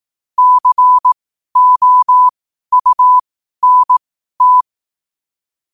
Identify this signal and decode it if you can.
Morse code, heard as a keyed tone, about 18 wpm, spaced as for 12 wpm, 1000 Hz. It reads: COUNT
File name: qotdfiles / 12wpm